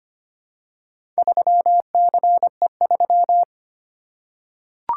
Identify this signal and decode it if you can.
Morse code, heard as a keyed tone, about 25 wpm, 700 Hz. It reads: 3CE3
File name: practice / 25wpm / Four Letter-Number Combinations - 25wpm